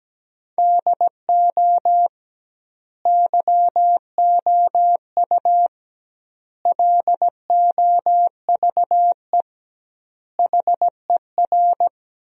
Morse code, heard as a keyed tone, about 17 wpm, 700 Hz. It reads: DO YOU LOVE HER
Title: 2x Word-Spacing - Sentences from Top 400 Words - 17wpm